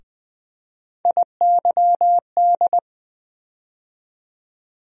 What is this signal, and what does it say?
Morse code, heard as a keyed tone, about 20 wpm, 700 Hz. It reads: IYD
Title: USA Call Sign Postfixes - 3 Letters 20wpm